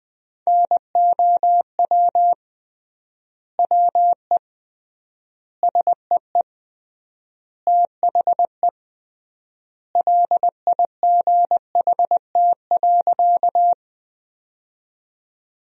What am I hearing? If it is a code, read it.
Morse code, heard as a keyed tone, about 20 wpm, 700 Hz. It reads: NOW WE SEE THE LIGHT.
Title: 3x Word-Spacing - Sentences from Top 200 Words - 20wpm